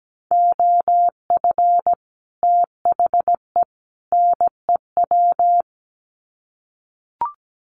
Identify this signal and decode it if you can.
Morse code, heard as a keyed tone, about 17 wpm, 700 Hz. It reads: OF THE NEW
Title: Sets of 3 Words 17wpm